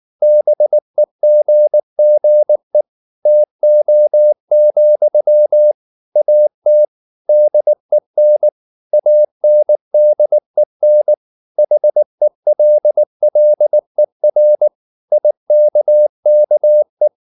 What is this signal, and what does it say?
Morse code, heard as a keyed tone, about 19 wpm, 600 Hz. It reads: BEGGE TO, AT DEN ANDEN HELLER IKKE
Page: Ferie 19 wpm